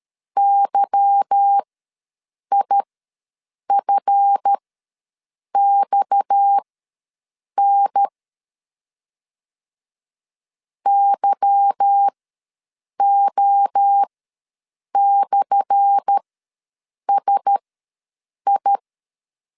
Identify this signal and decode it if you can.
Morse code, heard as a keyed tone, about 13 wpm, spaced as for 7 wpm, 800 Hz. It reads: YIFXN YO/SI
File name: LEZIONI / 10 SIGNS / 10A SLASH ws 12wpm 3443